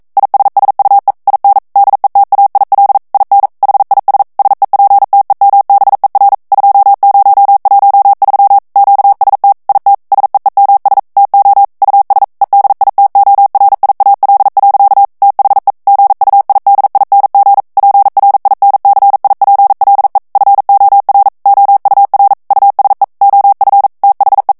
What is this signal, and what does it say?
Morse code, heard as a keyed tone, about 42 wpm, 800 Hz. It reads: ISSUE IN DETAIL IN HIS SEPTEMBER 2013 QST IT SEEMS TO US EDITORIAL. THE GUIDING PRINCIPLE FOR OUR USE OF THE